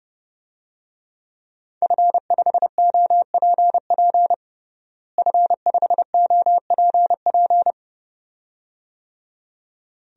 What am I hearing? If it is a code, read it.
Morse code, heard as a keyed tone, about 30 wpm, 700 Hz. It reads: F5OPP F5OPP
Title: DXCC-100 Call Signs - 5 Characters - Repeated 2x - 30wpm